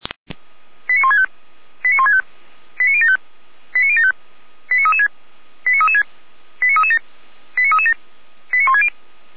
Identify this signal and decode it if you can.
Five-tone selective call (ZVEI), heard as quick runs of five tones, tones of about 80 ms. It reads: 89165 89165 89075 89075 89336 89336 89337 89337 89160